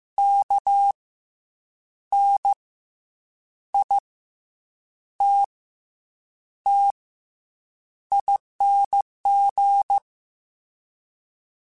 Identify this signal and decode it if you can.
Morse code, heard as a keyed tone, about 15 wpm, spaced as for 7 wpm, 800 Hz. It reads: KNITTING
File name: Special INGs-mp3s / Special INGs-mp3s / Fast ing @13